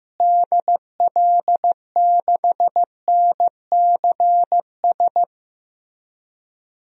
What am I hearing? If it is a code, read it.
Morse code, heard as a keyed tone, about 15 wpm, 700 Hz. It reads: DL6NCS
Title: DXCC-100 Call Signs - 6 Characters - Repeated 2x - 15wpm